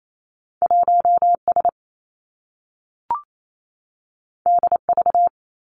Morse code, heard as a keyed tone, about 28 wpm, 700 Hz. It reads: 1 S D V
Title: Two Challenging Letter-Number Combinations 28wpm